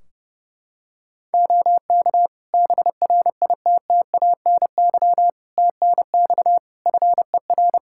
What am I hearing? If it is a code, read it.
Morse code, heard as a keyed tone, about 30 wpm, 700 Hz. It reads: OK BRITTANY TNX FER
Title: K1USN SST QSO Practice 30wpm